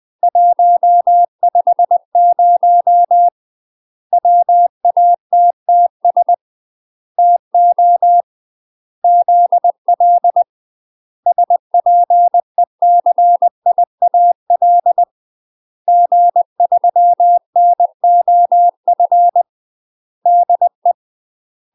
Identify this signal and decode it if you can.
Morse code, heard as a keyed tone, about 20 wpm, 700 Hz. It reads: 150 WATTS TO ZL SPECIAL G3NOF DE
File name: UK MASTER QSOs TRAINING / 20WPM / UK MASTER QSOs 0035